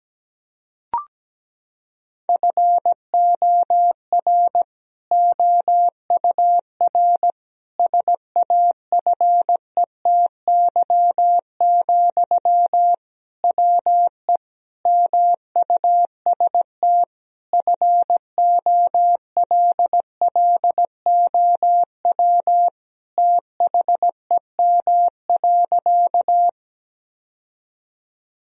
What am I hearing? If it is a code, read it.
Morse code, heard as a keyed tone, about 17 wpm, 700 Hz. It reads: FOR OUR SAFETY, WE MUST FOLLOW THEM.
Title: Sentences from Top 1000 Words - Review Part 4 - 17wpm